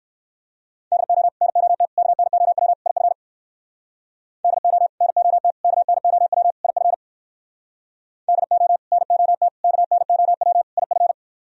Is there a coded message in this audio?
Yes: Morse code from 0.9 to 11.1 s, heard as a keyed tone, about 58 wpm, 700 Hz. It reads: DO NOT KNOW IF DO NOT KNOW IF DO NOT KNOW IF